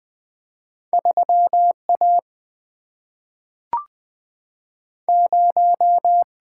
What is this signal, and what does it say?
Morse code, heard as a keyed tone, about 20 wpm, 700 Hz. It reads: 3 A 0
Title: Two Letter-Number Combinations - 20wpm